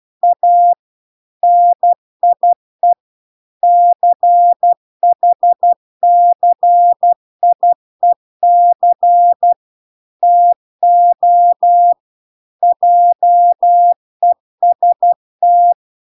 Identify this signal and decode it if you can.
Morse code, heard as a keyed tone, about 12 wpm, 700 Hz. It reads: A NIE CHCIEC TO JEST